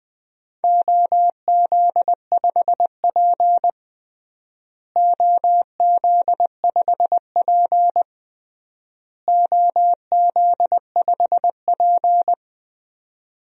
Morse code, heard as a keyed tone, about 20 wpm, 700 Hz. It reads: OZ5P OZ5P OZ5P
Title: DXCC-100 Call Signs - 4 Characters - Repeated 3x - 20wpm